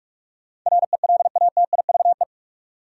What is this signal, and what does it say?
Morse code, heard as a keyed tone, about 44 wpm, 700 Hz. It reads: RELATIVE
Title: Sentences from Top 1400 Words - New Word Drill Alt - 45wpm